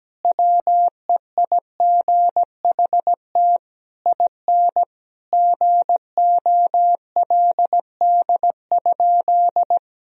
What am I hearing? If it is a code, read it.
Morse code, heard as a keyed tone, about 17 wpm, 700 Hz. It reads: WEIGHT IN GOLD?
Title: Sentences from Top 1000 Words - Review Part 3 - 17wpm